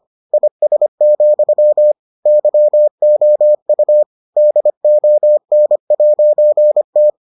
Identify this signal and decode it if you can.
Morse code, heard as a keyed tone, about 25 wpm, 600 Hz. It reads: IS, YOU DON'T